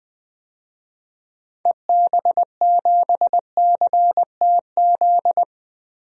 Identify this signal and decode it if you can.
Morse code, heard as a keyed tone, about 20 wpm, 700 Hz. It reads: EB7CTZ
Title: DXCC-100 Call Signs - 6 Characters - Repeated 3x - 20wpm